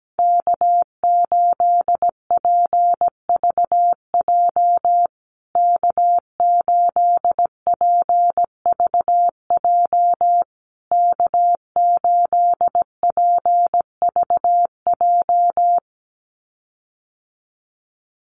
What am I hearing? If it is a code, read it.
Morse code, heard as a keyed tone, about 17 wpm, 700 Hz. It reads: K8PVJ K8PVJ K8PVJ
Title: USA Call Signs Repeated 3x 17wpm